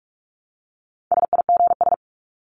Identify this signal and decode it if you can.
Morse code, heard as a keyed tone, about 44 wpm, 700 Hz. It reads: SIGS
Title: Abbreviations Repeated 3x - 45wpm